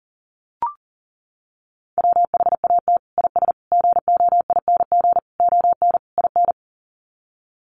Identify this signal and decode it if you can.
Morse code, heard as a keyed tone, about 40 wpm, 700 Hz. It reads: WHAT IS GOING ON IN
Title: Sets of 5 Words Encore2 40wpm